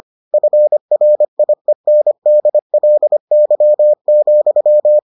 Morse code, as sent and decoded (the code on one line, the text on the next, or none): ..-. .-. .. -. -.. .-.. -.-- --..--
FRIENDLY,